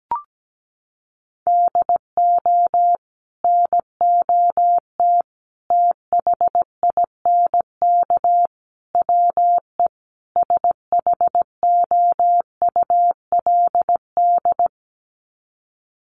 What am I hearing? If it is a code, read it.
Morse code, heard as a keyed tone, about 17 wpm, 700 Hz. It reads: DO NOT THINK WE SHOULD
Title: Sets of 5 Words Encore 17wpm